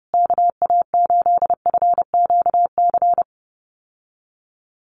Morse code, heard as a keyed tone, about 30 wpm, 700 Hz. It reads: KA8FQC
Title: USA Call Signs Repeated 3x 30wpm